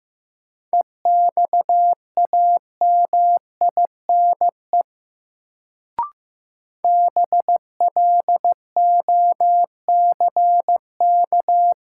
Morse code, heard as a keyed tone, about 15 wpm, 700 Hz. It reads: EXAMINE BLOCK
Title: Sentences from Top 1600 Words - New Word Drill - 15wpm